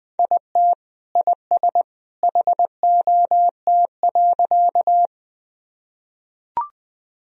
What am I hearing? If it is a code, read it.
Morse code, heard as a keyed tone, about 21 wpm, 700 Hz. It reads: IT IS HOT.
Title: Sentences from Top 1000 Words - Review Part 3 - 20wpm